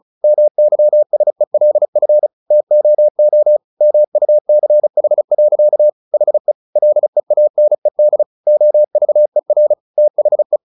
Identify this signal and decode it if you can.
Morse code, heard as a keyed tone, about 34 wpm, 600 Hz. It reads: MYSELF TOO MUCH. HE LEANED OVER THE
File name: the old man and the sea 2 35wpm